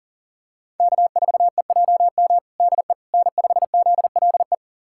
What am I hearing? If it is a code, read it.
Morse code, heard as a keyed tone, about 40 wpm, 700 Hz. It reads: K4EJM DE N5ZLE
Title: USA Callsign DE Callsign - 5 Characters - 40wpm